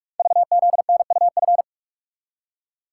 Morse code, heard as a keyed tone, about 44 wpm, 690 Hz. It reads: VZNUF
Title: Code Groups - 5 Letter:Numbers - 45wpm